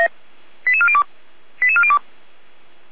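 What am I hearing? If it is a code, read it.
Five-tone selective call (ZVEI), heard as quick runs of five tones, tones of about 70 ms. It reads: 88402 88402
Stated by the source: ZVEI five-tone selective call 89276 received by SDR, logged on the Feuerwehr Orlamünde server.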